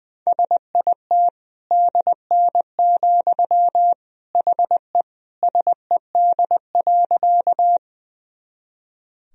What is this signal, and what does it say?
Morse code, heard as a keyed tone, about 20 wpm, 700 Hz. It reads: SIT DN, HE SED.
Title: Sentences from Top 500 Words and Abbreviations 20wpm